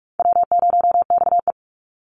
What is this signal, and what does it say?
Morse code, heard as a keyed tone, about 44 wpm, 700 Hz. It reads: W0XE
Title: Easy Call Signs Repeated 3x - 45wpm